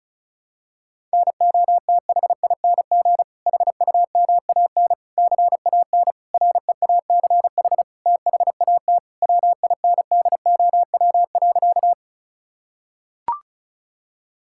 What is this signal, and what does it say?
Morse code, heard as a keyed tone, about 36 wpm, 700 Hz. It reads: NOTHING HUMAN CAN REACH THAT WINDOW.
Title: Sentences from Top 1600 Words 35wpm